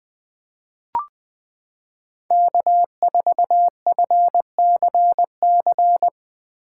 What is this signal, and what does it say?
Morse code, heard as a keyed tone, about 20 wpm, 700 Hz. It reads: K4FCC
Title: K1USN SST QSO Practice - 20wpm